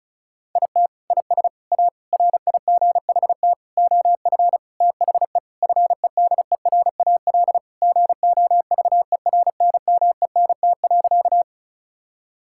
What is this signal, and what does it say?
Morse code, heard as a keyed tone, about 36 wpm, 700 Hz. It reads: IT IS A RIGHT OF THE FEDERAL GOVERNMENT.
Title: Sentences from Top 1000 Words - Review Part 5 - 35wpm